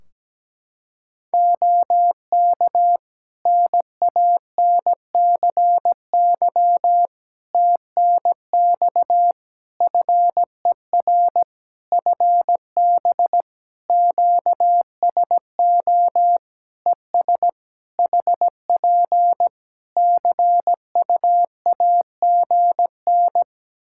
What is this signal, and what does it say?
Morse code, heard as a keyed tone, about 17 wpm, 700 Hz. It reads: OK NANCY TNX FER FB QSO ES HP CUAGN
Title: Practice QSO - Exchange 4 - Ending - 17wpm